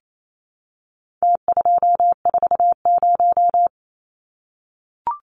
Morse code, heard as a keyed tone, about 28 wpm, 700 Hz. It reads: T240
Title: Numbers - Contest Style - 4-Digit 28wpm